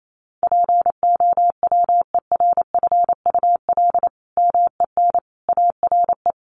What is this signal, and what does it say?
Morse code, heard as a keyed tone, about 28 wpm, 700 Hz. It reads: POWERFUL MEN ARE
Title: Sentences from Top 1400 Words 28wpm